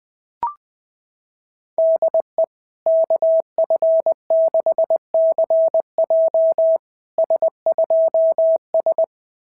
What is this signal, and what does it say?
Morse code, heard as a keyed tone, about 20 wpm, 650 Hz. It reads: DE KF6CJ S2S
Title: SOTA QSO Practice 20wpm